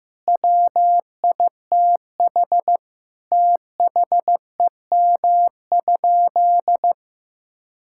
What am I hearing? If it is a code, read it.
Morse code, heard as a keyed tone, about 15 wpm, 700 Hz. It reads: WITH THEM?
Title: Sentences from Top 1000 Words - Review Part 6 - 15wpm